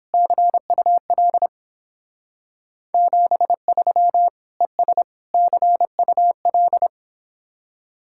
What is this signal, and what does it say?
Morse code, heard as a keyed tone, about 29 wpm, 700 Hz. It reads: CUL 73 ES CUL